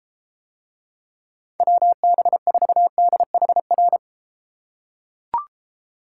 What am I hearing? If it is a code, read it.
Morse code, heard as a keyed tone, about 32 wpm, 700 Hz. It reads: WB4DHR